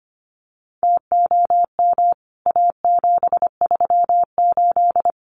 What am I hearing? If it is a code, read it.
Morse code, heard as a keyed tone, about 25 wpm, 700 Hz. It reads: TOM A738
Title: CWT Exchange 25wpm